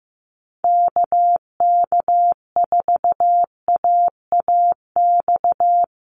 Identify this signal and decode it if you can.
Morse code, heard as a keyed tone, about 15 wpm, 700 Hz. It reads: KK4AAX
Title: USA Call Signs Repeated 3x 15wpm